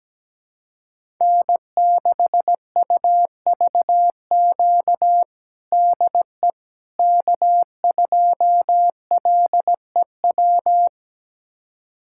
Morse code, heard as a keyed tone, about 17 wpm, 700 Hz. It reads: N6UVQ DE K2LEW
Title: USA Callsign DE Callsign - 5 Characters - 17wpm